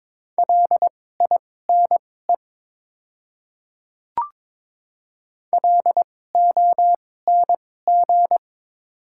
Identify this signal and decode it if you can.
Morse code, heard as a keyed tone, about 21 wpm, spaced as for 16 wpm, 700 Hz. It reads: LINE LONG
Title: 2x Character Spacing - Top 300 Words - 22wpm